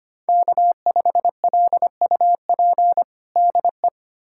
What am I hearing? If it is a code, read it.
Morse code, heard as a keyed tone, about 25 wpm, 700 Hz. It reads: K5LUP DE
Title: USA Callsign DE Callsign - 5 Characters - 25wpm